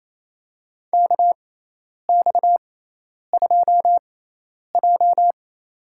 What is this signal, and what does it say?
Morse code, heard as a keyed tone, about 28 wpm, spaced as for 10 wpm, 700 Hz. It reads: KX2J
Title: USA Call Signs - 4 Characters - 6x Character Spacing - 28wpm